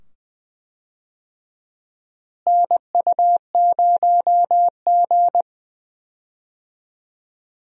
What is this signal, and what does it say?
Morse code, heard as a keyed tone, about 20 wpm, 700 Hz. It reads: NU0G
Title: USA Call Signs - 4 Characters - Repeated 2x - 20wpm